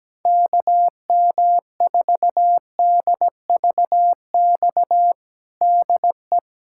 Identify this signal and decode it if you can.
Morse code, heard as a keyed tone, about 17 wpm, 700 Hz. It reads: KM4DVX DE
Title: USA Callsign DE Callsign 17wpm